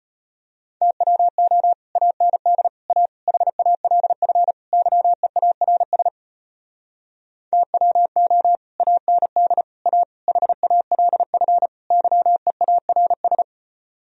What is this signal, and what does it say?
Morse code, heard as a keyed tone, about 33 wpm, 700 Hz. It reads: TWO AND A HALF YEARS TWO AND A HALF YEARS